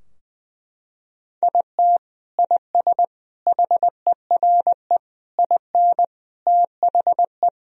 Morse code, heard as a keyed tone, about 20 wpm, 700 Hz. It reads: IT IS HERE IN THE
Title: Sentences from Top 1000 Words - Review Part 3 - 20wpm